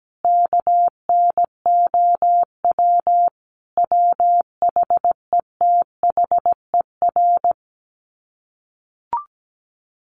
Morse code, heard as a keyed tone, about 17 wpm, 700 Hz. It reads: KNOW WHETHER
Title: Sets of 5 Words 17wpm